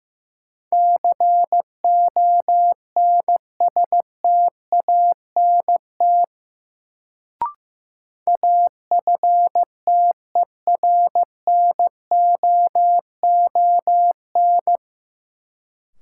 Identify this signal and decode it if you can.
Morse code, heard as a keyed tone, about 15 wpm, 700 Hz. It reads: CONSTANT AFTERNOON
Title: Sentences from Top 1500 Words - New Word Drill - 15wpm